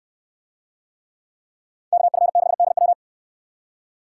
Morse code, heard as a keyed tone, about 63 wpm, 700 Hz. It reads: KF6DW